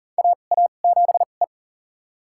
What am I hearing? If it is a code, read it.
Morse code, heard as a keyed tone, about 37 wpm, spaced as for 28 wpm, 700 Hz. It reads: AA7E
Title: USA Call Signs - 2x Character Spacing - 40wpm